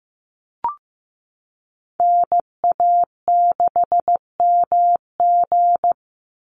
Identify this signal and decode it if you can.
Morse code, heard as a keyed tone, about 15 wpm, 700 Hz. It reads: NA6MG
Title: NA SOTA Chaser Callsigns 15wpm